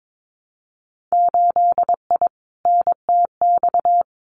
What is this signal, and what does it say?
Morse code, heard as a keyed tone, about 22 wpm, 700 Hz. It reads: 8I NTX
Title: Winter Field Day Exchanges Encore 22wpm